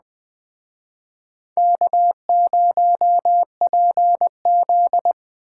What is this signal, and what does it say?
Morse code, heard as a keyed tone, about 21 wpm, 700 Hz. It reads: K0PZ